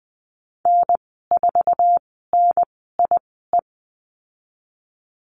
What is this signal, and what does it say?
Morse code, heard as a keyed tone, about 20 wpm, spaced as for 14 wpm, 700 Hz. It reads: N4NIE
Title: 069 Lesson 12g - Letter d - Call Signs 20wpm